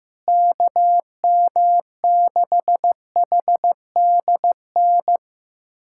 Morse code, heard as a keyed tone, about 16 wpm, 700 Hz. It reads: KM6HDN
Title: USA Callsign DE Callsign - 6 Characters - 15wpm